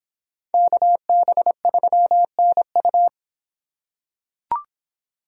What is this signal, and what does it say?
Morse code, heard as a keyed tone, about 23 wpm, 700 Hz. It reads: KB3NU